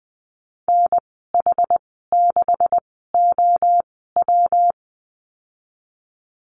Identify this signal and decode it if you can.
Morse code, heard as a keyed tone, about 19 wpm, spaced as for 14 wpm, 700 Hz. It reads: NH6OW
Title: 128 Lesson 21g - Number 6 - Call Signs 20wpm